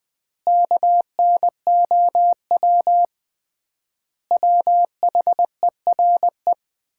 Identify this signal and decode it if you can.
Morse code, heard as a keyed tone, about 20 wpm, 700 Hz. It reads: KNOW WHERE
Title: 3x Word-Spacing - Sets of 5 Words - 20wpm